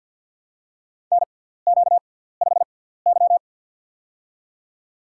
Morse code, heard as a keyed tone, about 53 wpm, spaced as for 17 wpm, 700 Hz. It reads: NQ5Y
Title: USA Call Signs - 6x Character Spacing - 50wpm